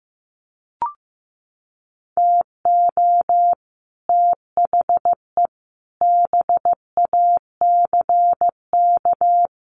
Morse code, heard as a keyed tone, about 14 wpm, 700 Hz. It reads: TO THE BACK
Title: Sets of 5 Words 15wpm